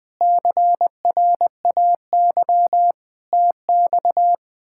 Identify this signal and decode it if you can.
Morse code, heard as a keyed tone, about 20 wpm, 700 Hz. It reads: CRAY TX
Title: K1USN SST QSO Practice - 20wpm